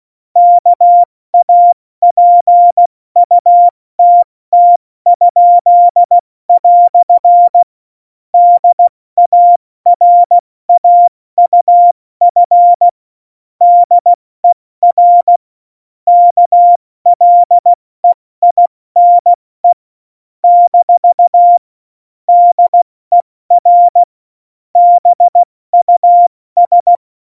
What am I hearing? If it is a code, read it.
Morse code, heard as a keyed tone, about 16 wpm, 700 Hz. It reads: KAPUTT?" DARAUF DER KLEINE - DER BUS
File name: Morsekurs / Witze / Witz 29. Januar 2019, 70 F80